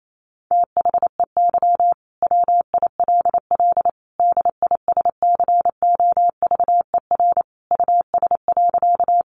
Morse code, heard as a keyed tone, about 28 wpm, 700 Hz. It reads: THEY WILL DISCOVER US.
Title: Sentences from Top 1100 Words 28wpm